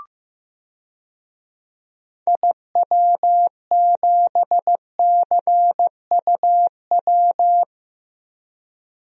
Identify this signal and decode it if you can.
Morse code, heard as a keyed tone, about 15 wpm, 700 Hz. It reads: IW7CUW
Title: DXCC-100 Call Signs - 6 Characters - 15wpm